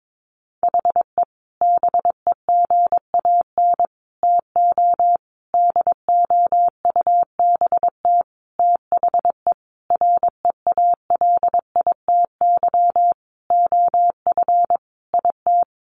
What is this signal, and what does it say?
Morse code, heard as a keyed tone, about 22 wpm, 700 Hz. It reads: HE BEGAN TO DOUBT THE REALITY OF IT
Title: Sentences from Top 1500 Words 22wpm